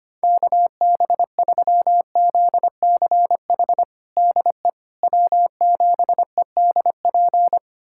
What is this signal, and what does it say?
Morse code, heard as a keyed tone, about 25 wpm, 700 Hz. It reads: KB3ZCH DE W7EDP
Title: USA Callsign DE Callsign 25wpm